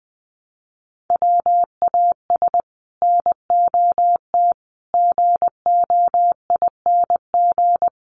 Morse code, heard as a keyed tone, about 20 wpm, 700 Hz. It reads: WAS NOT GOING
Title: Sets of 4 Words 20wpm